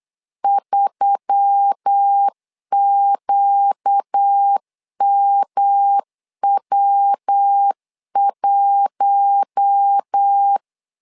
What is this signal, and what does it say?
Morse code, heard as a keyed tone, about 8 wpm, 800 Hz. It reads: 3QMW1
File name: LEZIONI / 4 Q1S3 8wpm 3037